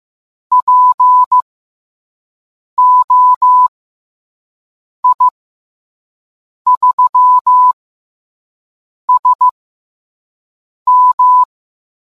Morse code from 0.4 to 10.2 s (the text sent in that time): P O I 3 S